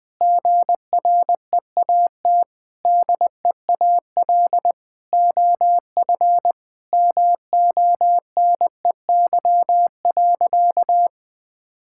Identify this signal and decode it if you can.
Morse code, heard as a keyed tone, about 19 wpm, 700 Hz. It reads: GREAT DEAL OF MONEY.
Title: Sentences from Top 1000 Words - Review Part 6 - 20wpm